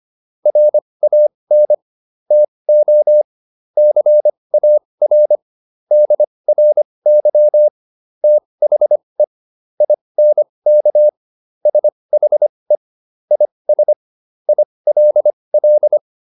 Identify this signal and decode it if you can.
Morse code, heard as a keyed tone, about 25 wpm, spaced as for 20 wpm, 600 Hz. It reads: RAN TO CAR DRY THE INK SHE IS ILL